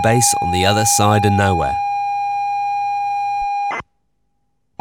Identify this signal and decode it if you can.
Computer data on cassette, with speech.